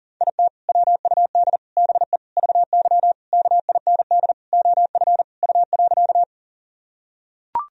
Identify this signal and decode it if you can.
Morse code, heard as a keyed tone, about 40 wpm, 700 Hz. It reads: IT WUD BE VY KIND OF U.
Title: Sentences from Top 200 Words and Abbreviations 40wpm